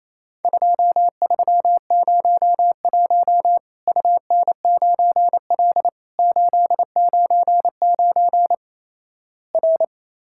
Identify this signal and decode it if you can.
Morse code, heard as a keyed tone, about 28 wpm, 700 Hz. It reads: 2301 UN9L 899 R
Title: Worked All Europe DX Contest - QTC Exchange - 28wpm